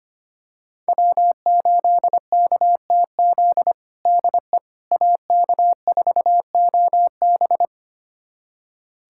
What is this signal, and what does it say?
Morse code, heard as a keyed tone, about 25 wpm, 700 Hz. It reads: W8KTZ DE AK4OB
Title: USA Callsign DE Callsign - 5 Characters - 25wpm